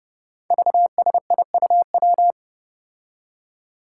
Morse code, heard as a keyed tone, about 30 wpm, 700 Hz. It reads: VSIUW
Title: Five Challenging Letter-Number Combinations 30wpm